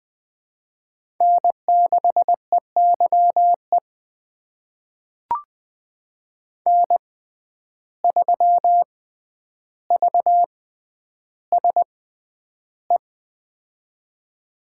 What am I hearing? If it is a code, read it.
Morse code, heard as a keyed tone, about 20 wpm, spaced as for 10 wpm, 700 Hz. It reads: N6EYE N3VSE
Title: USA Call Signs - 5 Characters - 6x Character Spacing - 20wpm